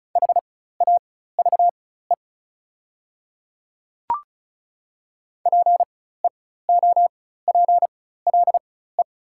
Morse code, heard as a keyed tone, about 36 wpm, spaced as for 16 wpm, 700 Hz. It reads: HAVE PEOPLE
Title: 4x Character Spacing - Top 100 Words - 35wpm